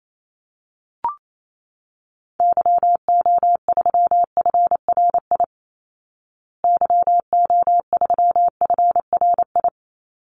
Morse code, heard as a keyed tone, about 28 wpm, 700 Hz. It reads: YO3FRI YO3FRI
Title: Call Signs Repeated 3x - 28wpm